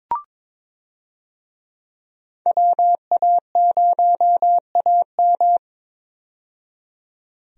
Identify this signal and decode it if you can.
Morse code, heard as a keyed tone, about 21 wpm, 700 Hz. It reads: WA0AM
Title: USA Call Signs - 5 Characters - 22wpm